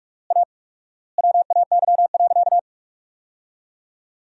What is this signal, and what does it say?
Morse code, heard as a keyed tone, about 46 wpm, 700 Hz. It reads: A WAY.
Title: 4x Word-Spacing - Sentences from Top 500 Words - 45wpm